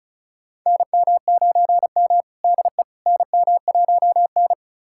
Morse code, heard as a keyed tone, about 36 wpm, 700 Hz. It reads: NM9M DE NM1N
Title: USA Callsign DE Callsign - 4 Characters - 35wpm